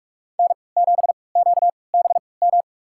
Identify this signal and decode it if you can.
Morse code, heard as a keyed tone, about 47 wpm, spaced as for 26 wpm, 700 Hz. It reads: N7QBM
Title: USA Call Signs - 5 Characters - 3x Character Spacing - 45wpm